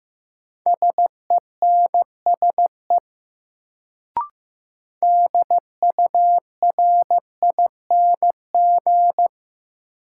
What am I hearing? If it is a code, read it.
Morse code, heard as a keyed tone, about 15 wpm, 700 Hz. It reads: SENSE DURING
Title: Sentences from Top 700 Words - New Word Drill - 15wpm